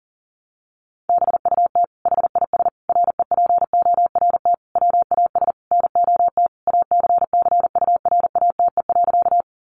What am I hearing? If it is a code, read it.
Morse code, heard as a keyed tone, about 40 wpm, 700 Hz. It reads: BUT HIS REPORT WAS NOT ACCURATE.
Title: Sentences from Top 1200 Words 40wpm